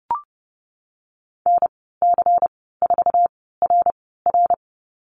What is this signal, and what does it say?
Morse code, heard as a keyed tone, about 30 wpm, spaced as for 17 wpm, 700 Hz. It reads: NC4RR
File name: Lesson 19g - Letter w - Call Signs 30wpm